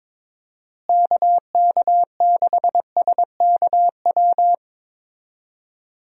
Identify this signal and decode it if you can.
Morse code, heard as a keyed tone, about 22 wpm, 700 Hz. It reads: KK6SKW
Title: USA Call Signs - 6 Characters - Repeated 3x - 22wpm